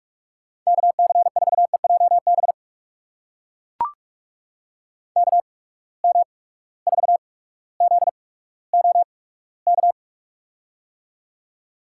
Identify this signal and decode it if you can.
Morse code, heard as a keyed tone, about 43 wpm, spaced as for 19 wpm, 700 Hz. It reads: KC4EJB KM4ZOK
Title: USA Call Signs - 6 Characters - 8x Character Spacing - 45wpm